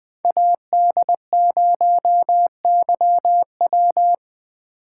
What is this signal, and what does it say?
Morse code, heard as a keyed tone, about 20 wpm, 700 Hz. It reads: AD0YW